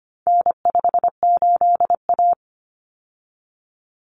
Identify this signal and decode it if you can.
Morse code, heard as a keyed tone, about 25 wpm, 700 Hz. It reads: N58A
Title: Common Cut Numbers - 4-Digit 25wpm